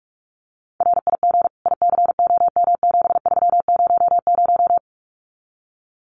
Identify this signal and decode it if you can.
Morse code, heard as a keyed tone, about 44 wpm, 700 Hz. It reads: RIG ICOM7300
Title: Common QSO Phrases 45wpm